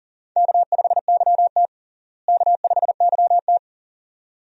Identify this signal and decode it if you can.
Morse code, heard as a keyed tone, about 40 wpm, 700 Hz. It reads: K5YT K5YT
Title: USA Call Signs - 4 Characters - Repeated 2x - 40wpm